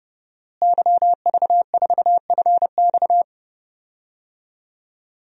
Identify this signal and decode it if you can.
Morse code, heard as a keyed tone, about 32 wpm, 700 Hz. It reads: YV4FX
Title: DXCC-100 Call Signs - 5 Characters - Repeated 3x - 30wpm